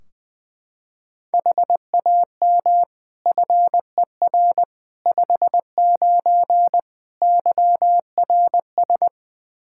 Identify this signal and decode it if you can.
Morse code, heard as a keyed tone, about 20 wpm, 700 Hz. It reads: HAM FER 59 YRS
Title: Practice QSO - Exchange 3 - More Info - 20wpm